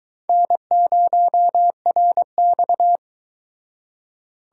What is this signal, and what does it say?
Morse code, heard as a keyed tone, about 22 wpm, 700 Hz. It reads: N0RX